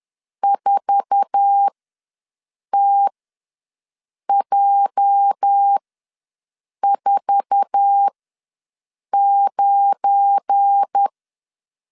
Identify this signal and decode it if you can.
Morse code, heard as a keyed tone, about 10 wpm, spaced as for 6 wpm, 800 Hz. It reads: 4TJ49
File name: LEZIONI / 8 YJ45 / 8 YJ4 ws 10wpm 4020